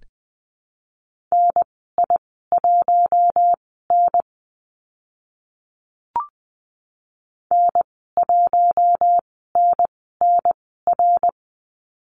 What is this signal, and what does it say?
Morse code, heard as a keyed tone, about 20 wpm, spaced as for 14 wpm, 700 Hz. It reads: NI1N N1NNR
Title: 055 Lesson 10g - Letter r - Call Signs 20wpm